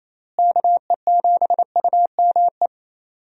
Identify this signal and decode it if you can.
Morse code, heard as a keyed tone, about 28 wpm, 700 Hz. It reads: KE7UME
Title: USA Call Signs - 6 Characters - Repeated 2x - 28wpm